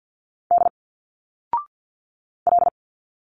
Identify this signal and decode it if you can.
Morse code, heard as a keyed tone, about 50 wpm, 700 Hz. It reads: D L
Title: Single Letters 50wpm